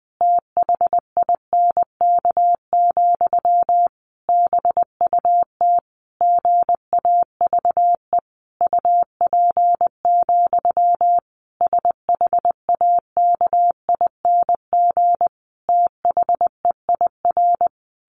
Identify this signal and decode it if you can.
Morse code, heard as a keyed tone, about 20 wpm, 700 Hz. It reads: THINK, BUT GAVE UP, SHAKING THEIR
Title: A Sound Of Thunder 20wpm